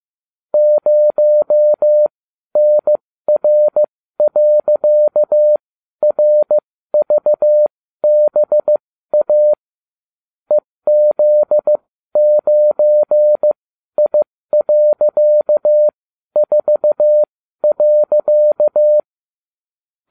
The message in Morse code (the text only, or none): 0NR.RVBA EZ9I.4.